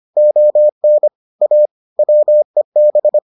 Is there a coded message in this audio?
Yes: Morse code from 0.2 to 3.2 s, heard as a keyed tone, about 25 wpm, 600 Hz. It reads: ON A WEB